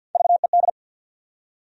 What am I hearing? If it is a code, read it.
Morse code, heard as a keyed tone, about 53 wpm, 700 Hz. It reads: VED
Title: USA Call Sign Postfixes - 3 Letters 50wpm